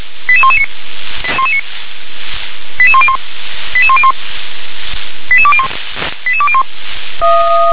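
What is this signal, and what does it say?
Five-tone selective call (ZVEI), heard as quick runs of five tones, tones of about 70 ms. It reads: 88119 88101 88101 88201 88201
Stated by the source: ZVEI five-tone selective call received by SDR.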